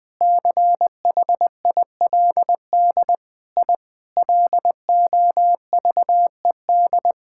Morse code, heard as a keyed tone, about 21 wpm, 700 Hz. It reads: CHILD I LOVED
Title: Sentences with 3-Word Phrases - Spoken Beginning - 20wpm